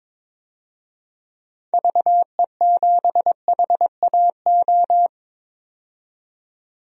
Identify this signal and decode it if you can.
Morse code, heard as a keyed tone, about 21 wpm, 700 Hz. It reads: VE7HAO
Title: DXCC-100 Call Signs - 6 Characters - 22wpm